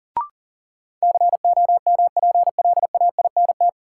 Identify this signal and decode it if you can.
Morse code, heard as a keyed tone, about 38 wpm, 700 Hz. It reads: COMPLAINT
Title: Sentences from Top 1400 Words - New Word Drill Alt - 40wpm